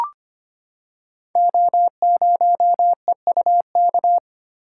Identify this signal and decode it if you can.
Morse code, heard as a keyed tone, about 25 wpm, 700 Hz. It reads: O0EUK
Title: Code Groups - 5 Letter:Numbers - 25wpm